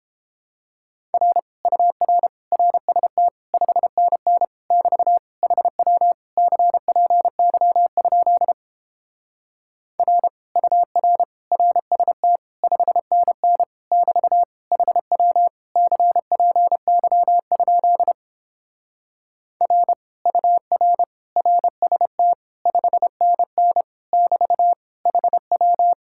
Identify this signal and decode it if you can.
Morse code, heard as a keyed tone, about 31 wpm, 700 Hz. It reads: R UR RST 5NN <BT> HW CPY? R UR RST 5NN <BT> HW CPY? R UR RST 5NN <BT> HW